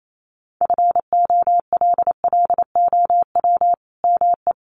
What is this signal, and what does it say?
Morse code, heard as a keyed tone, about 28 wpm, 700 Hz. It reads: FOLLOW ME
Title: Sentences from Top 1400 Words 28wpm